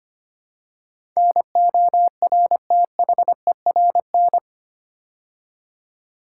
Morse code, heard as a keyed tone, about 25 wpm, 690 Hz. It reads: NORTHERN
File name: 8 Letter Words - First 3 Letters Given - 25wpm